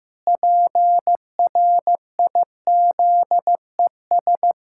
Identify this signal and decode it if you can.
Morse code, heard as a keyed tone, about 15 wpm, 700 Hz. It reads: PRIZES
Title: Pangram Challenge v2 - 15wpm